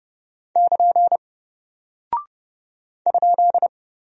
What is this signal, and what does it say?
Morse code, heard as a keyed tone, about 30 wpm, 700 Hz. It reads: <KN> ?
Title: Single Character (Letter:Number:Punctuation:Prosign) 30wpm